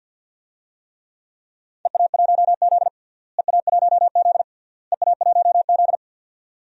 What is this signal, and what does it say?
Morse code, heard as a keyed tone, about 51 wpm, 700 Hz. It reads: EA1Z EA1Z EA1Z
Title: DXCC-100 Call Signs - 4 Characters - Repeated 3x - 50wpm